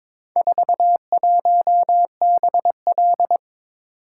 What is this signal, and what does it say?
Morse code, heard as a keyed tone, about 21 wpm, 700 Hz. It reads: 41BL
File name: practice / 22wpm / Four Letter-Number Combinations - 22wpm